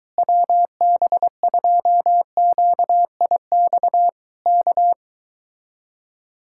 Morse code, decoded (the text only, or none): WB2QIX K